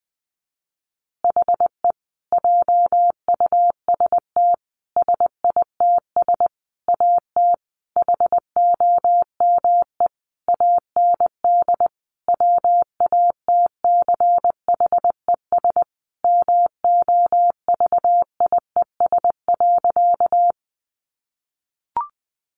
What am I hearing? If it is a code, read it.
Morse code, heard as a keyed tone, about 20 wpm, 700 Hz. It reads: HE JUST SITS AT HOME AND WATCHES MOVIES.
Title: Additional Easy Phrases 20wpm